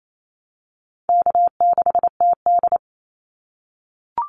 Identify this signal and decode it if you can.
Morse code, heard as a keyed tone, about 28 wpm, 700 Hz. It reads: K6TD
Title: Easy Call Signs - 28wpm